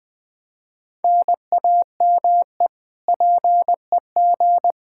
Morse code, heard as a keyed tone, about 20 wpm, 700 Hz. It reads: NAME PEG
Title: Practice QSO - Exchange 1 - Beginning - 20wpm